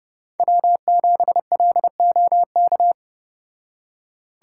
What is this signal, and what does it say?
Morse code, heard as a keyed tone, about 35 wpm, 700 Hz. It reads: W7LOK